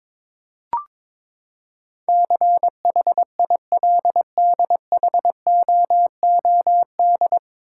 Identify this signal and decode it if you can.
Morse code, heard as a keyed tone, about 21 wpm, 700 Hz. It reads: CHILDHOOD
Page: Morse Code Ninja practice 22wpm